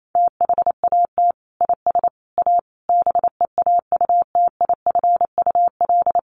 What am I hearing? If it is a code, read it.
Morse code, heard as a keyed tone, about 28 wpm, 700 Hz. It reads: THAT IS A BEAUTIFUL
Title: Sentences from Top 1500 Words 28wpm